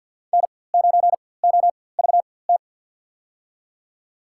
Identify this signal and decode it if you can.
Morse code, heard as a keyed tone, about 53 wpm, spaced as for 24 wpm, 700 Hz. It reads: N9OVT
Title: USA Call Signs - 5 Characters - 4x Character Spacing - 50wpm